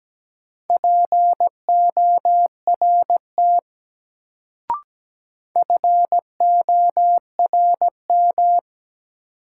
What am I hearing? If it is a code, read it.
Morse code, heard as a keyed tone, about 17 wpm, 700 Hz. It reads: PORT FORM
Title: Sentences from Top 1000 Words - Review Part 1 - New Word Drill - 17wpm